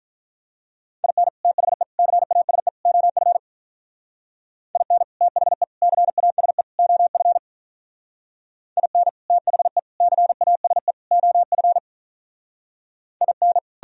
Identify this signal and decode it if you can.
Morse code, heard as a keyed tone, about 47 wpm, 700 Hz. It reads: IN THE CASE OF IN THE CASE OF IN THE CASE OF IN